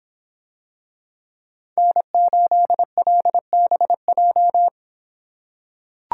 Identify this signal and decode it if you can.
Morse code, heard as a keyed tone, about 26 wpm, 700 Hz. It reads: N8LBJ